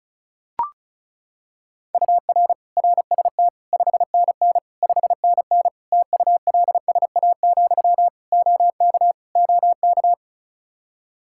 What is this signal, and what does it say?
Morse code, heard as a keyed tone, about 36 wpm, 700 Hz. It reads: UR RST 5NN 5NN TULSA, OK OK
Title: Practice QSO - Exchange 1 - Beginning - 35wpm